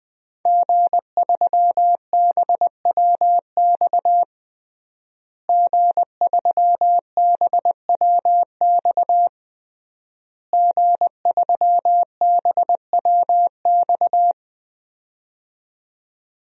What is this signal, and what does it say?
Morse code, heard as a keyed tone, about 20 wpm, 700 Hz. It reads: G3BWX G3BWX G3BWX
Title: DXCC-100 Call Signs - 5 Characters - Repeated 3x - 20wpm